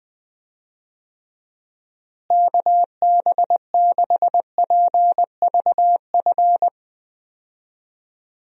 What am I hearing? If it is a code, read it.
Morse code, heard as a keyed tone, about 20 wpm, 700 Hz. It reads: KB6PVF
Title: USA Call Signs - 6 Characters - Repeated 2x - 20wpm